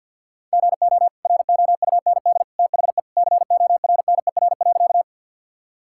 Morse code, heard as a keyed tone, about 50 wpm, 700 Hz. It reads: GO ROUND THE CORNER.
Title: Sentences from Top 1600 Words 50wpm